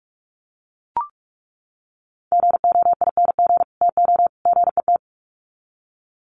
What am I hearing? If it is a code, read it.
Morse code, heard as a keyed tone, about 44 wpm, 700 Hz. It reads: GOING TO GET
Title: Sets of 3 Words 45wpm